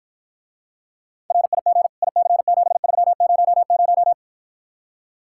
Morse code, heard as a keyed tone, about 54 wpm, 700 Hz. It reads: RIG IC7300